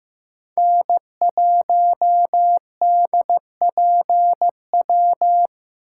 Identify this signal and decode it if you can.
Morse code, heard as a keyed tone, about 15 wpm, 700 Hz. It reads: N1DPW
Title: USA Call Signs - 8x Character Spacing - 15wpm